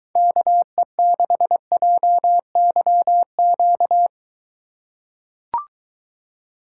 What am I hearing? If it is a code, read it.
Morse code, heard as a keyed tone, about 22 wpm, 700 Hz. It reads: KE6JYQ